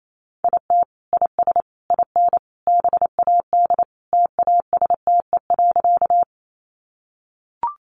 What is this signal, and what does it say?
Morse code, heard as a keyed tone, about 28 wpm, 700 Hz. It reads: IT IS IN BAD TASTE.
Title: Sentences from Top 1100 Words 28wpm